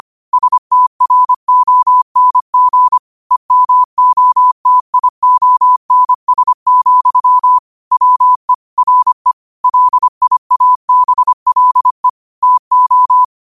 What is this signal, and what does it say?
Morse code, heard as a keyed tone, about 25 wpm, 1000 Hz. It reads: STRONG EMOTIONS, WE RE LIABLE TO
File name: qotdfiles / 25wpm